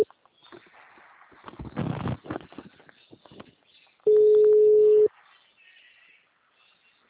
Ringback tone (Europe, 425 Hz).